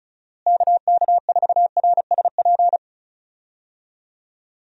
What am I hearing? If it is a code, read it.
Morse code, heard as a keyed tone, about 34 wpm, 700 Hz. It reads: KK4RSP
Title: USA Call Signs - 6 Characters - Repeated 2x - 35wpm